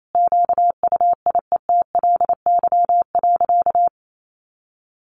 Morse code, heard as a keyed tone, about 28 wpm, 700 Hz. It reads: QUIETLY.
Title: A Sound Of Thunder 28wpm